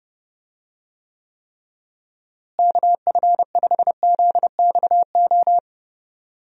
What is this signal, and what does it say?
Morse code, heard as a keyed tone, about 30 wpm, 700 Hz. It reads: KF5ZXO
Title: USA Call Signs - 6 Characters - Repeated 3x - 30wpm